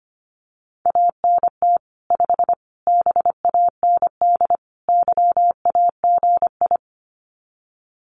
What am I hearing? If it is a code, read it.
Morse code, heard as a keyed tone, about 25 wpm, 700 Hz. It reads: ANT 5 BAND YAGI
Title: Common QSO Phrases 25wpm